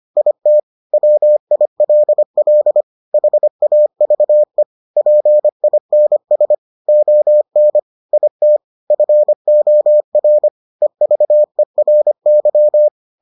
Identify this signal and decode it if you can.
Morse code, heard as a keyed tone, about 25 wpm, 600 Hz. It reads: IT WILL HAVE PINS ON IT FOR EVERY